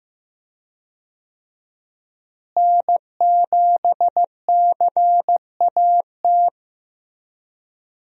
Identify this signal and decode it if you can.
Morse code, heard as a keyed tone, about 15 wpm, 700 Hz. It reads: N7CAT
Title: USA Call Signs - 5 Characters - Repeated 3x - 15wpm